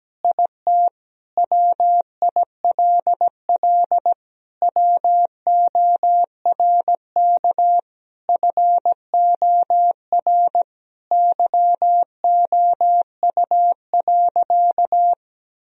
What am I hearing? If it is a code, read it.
Morse code, heard as a keyed tone, about 17 wpm, 700 Hz. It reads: IT WILL WORK FOR YOU.
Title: Sentences from Top 1000 Words - Review Part 2 - 17wpm